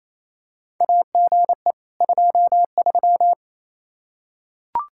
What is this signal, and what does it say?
Morse code, heard as a keyed tone, about 28 wpm, 700 Hz. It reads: AGE 23
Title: Practice QSO - Exchange 3 - More Info - 28wpm